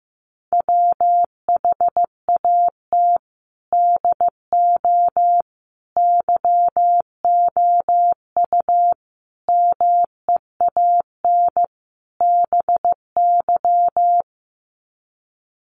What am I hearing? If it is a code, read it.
Morse code, heard as a keyed tone, about 15 wpm, 700 Hz. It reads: WHAT DO YOU MEAN BY
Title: Sets of 5 Words 15wpm